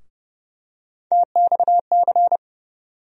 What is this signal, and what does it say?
Morse code, heard as a keyed tone, about 30 wpm, 700 Hz. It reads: TXC
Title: USA Call Sign Postfixes - 3 Letters 30wpm